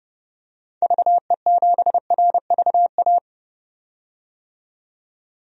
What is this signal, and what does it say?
Morse code, heard as a keyed tone, about 32 wpm, 700 Hz. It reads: VE7RVA
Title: DXCC-100 Call Signs - 6 Characters - 30wpm